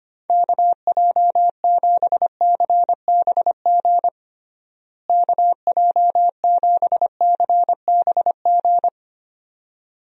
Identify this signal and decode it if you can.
Morse code, heard as a keyed tone, about 25 wpm, 700 Hz. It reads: KJ7CBG KJ7CBG
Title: USA Call Signs - 6 Characters - Repeated 2x - 25wpm